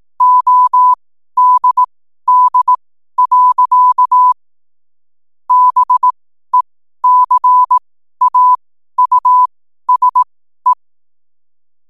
Morse code, heard as a keyed tone, about 18 wpm, spaced as for 12 wpm, 1000 Hz. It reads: ODD. BECAUSE
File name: n7net / 12wpm